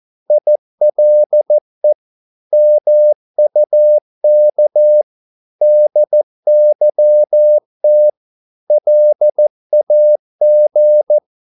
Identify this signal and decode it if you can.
Morse code, heard as a keyed tone, about 14 wpm, 600 Hz. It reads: ILE MUK DYT LAG
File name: Korte ord DK 14wpm